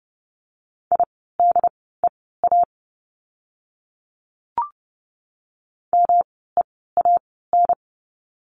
Morse code, heard as a keyed tone, about 30 wpm, spaced as for 17 wpm, 700 Hz. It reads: IDEA MEAN
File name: course / 30:10wpm / 108 Lesson 18f - Letter m - Words 30wpm